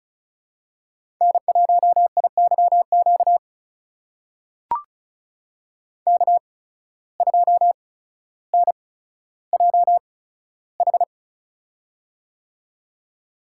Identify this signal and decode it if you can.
Morse code, heard as a keyed tone, about 36 wpm, spaced as for 15 wpm, 700 Hz. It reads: N1IYQ K2NJH
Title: USA Call Signs - 5 Characters - 8x Character Spacing - 35wpm